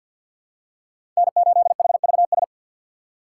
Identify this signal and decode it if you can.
Morse code, heard as a keyed tone, about 52 wpm, 700 Hz. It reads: N8HVS
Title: USA Call Signs - 5 Characters - 4x Character Spacing - 50wpm